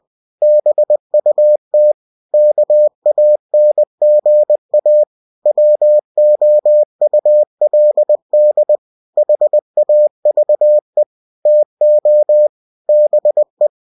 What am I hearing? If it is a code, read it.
Morse code, heard as a keyed tone, about 20 wpm, 600 Hz. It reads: BUT KANGA WOULD HAVE TO BE
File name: cw-books / winnie-the-pooh / winnie-20wpm-page-0064